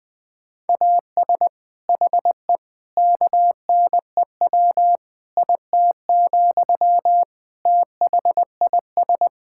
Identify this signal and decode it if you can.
Morse code, heard as a keyed tone, about 21 wpm, 700 Hz. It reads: AS HE KNEW IT, THIS
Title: Sentences from Top 1600 Words 20wpm